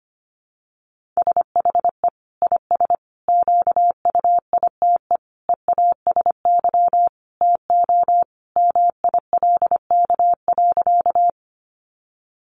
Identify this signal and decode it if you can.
Morse code, heard as a keyed tone, about 25 wpm, 700 Hz. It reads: SHE IS QUITE EASY TO MILK.
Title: Sentences from Top 1500 Words 25wpm